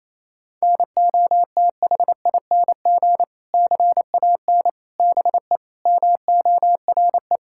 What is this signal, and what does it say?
Morse code, heard as a keyed tone, about 27 wpm, 700 Hz. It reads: NOTHING CAN BE MORE
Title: Sentences from Top 1600 Words 28wpm